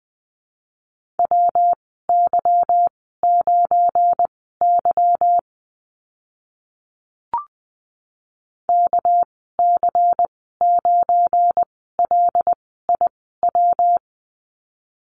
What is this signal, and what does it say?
Morse code, heard as a keyed tone, about 19 wpm, spaced as for 14 wpm, 700 Hz. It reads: WY9Y KC9LIW
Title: 200 Lesson 32g - Letter k - Call Signs 20wpm